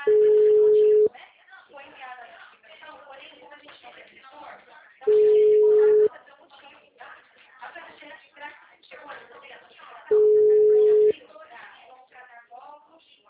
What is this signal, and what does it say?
Ringback tone (Europe, 425 Hz), with speech.